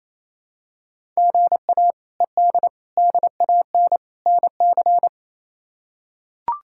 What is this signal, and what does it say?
Morse code, heard as a keyed tone, about 28 wpm, 700 Hz. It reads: GA ED DAN NC
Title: K1USN SST QSO Practice - 28wpm